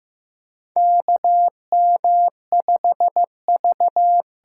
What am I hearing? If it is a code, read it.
Morse code, heard as a keyed tone, about 15 wpm, 700 Hz. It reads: KM5V
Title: USA Call Signs - 8x Character Spacing - 15wpm